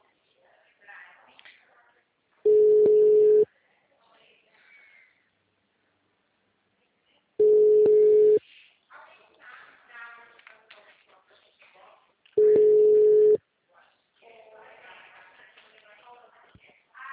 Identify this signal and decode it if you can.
Ringback tone (Europe, 425 Hz), with speech.
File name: out- -2012- - - .762